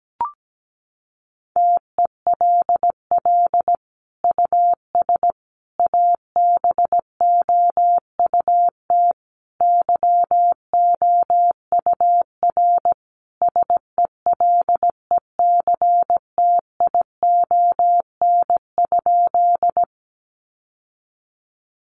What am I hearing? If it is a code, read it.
Morse code, heard as a keyed tone, about 17 wpm, 700 Hz. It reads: TELL US ABOUT YOUR SELECTION?
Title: Sentences from Top 1400 Words 17wpm